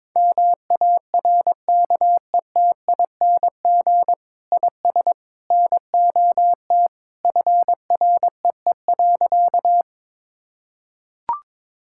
Morse code, heard as a keyed tone, about 22 wpm, 700 Hz. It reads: MARKETING IS NOT FREE.
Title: Sentences from Top 1000 Words - Review Part 5 - 22wpm